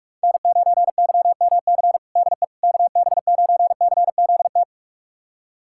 Morse code, heard as a keyed tone, about 46 wpm, 700 Hz. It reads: N9YMC DE KB9CZT
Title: USA Callsign DE Callsign 45wpm